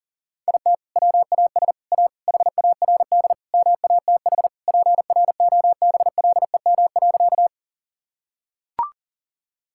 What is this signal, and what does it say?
Morse code, heard as a keyed tone, about 40 wpm, 700 Hz. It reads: IT WAS A HARD MATH PROBLEM.
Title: Sentences from Top 1000 Words - Review Part 6 - 40wpm